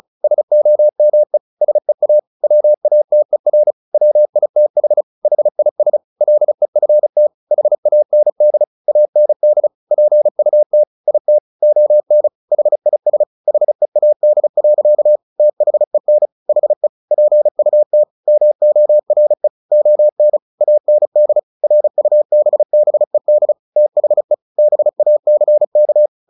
Morse code, heard as a keyed tone, about 34 wpm, 600 Hz. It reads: SOME SEA WATER WITH HIS LEFT HAND AND PUT IT ON HIS HEAD. THEN HE PUT MORE ON AND RUBBED THE BACK
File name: the old man and the sea 2 35wpm